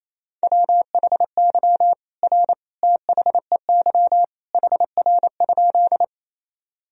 Morse code, heard as a keyed tone, about 27 wpm, 700 Hz. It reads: WHY R THEY HR?
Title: Sentences from Top 200 Words and Abbreviations 28wpm